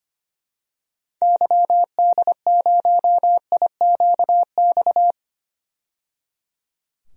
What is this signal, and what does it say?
Morse code, heard as a keyed tone, about 25 wpm, 700 Hz. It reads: YD0IQX
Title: DXCC-100 Call Signs - 6 Characters - 25wpm